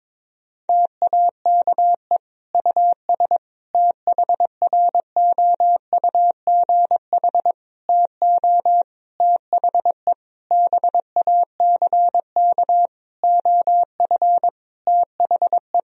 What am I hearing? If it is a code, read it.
Morse code, heard as a keyed tone, about 22 wpm, 700 Hz. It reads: TAKE US THROUGH TO THE BACK OF THE
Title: Sentences from Top 1000 Words - Review Part 1 - 22wpm